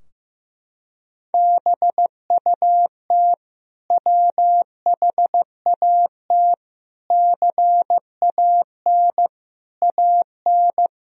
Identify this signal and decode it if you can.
Morse code, heard as a keyed tone, about 16 wpm, 700 Hz. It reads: BUT WHAT CAN AN
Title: Sentences from Top 1000 Words - Review Part 4 - 15wpm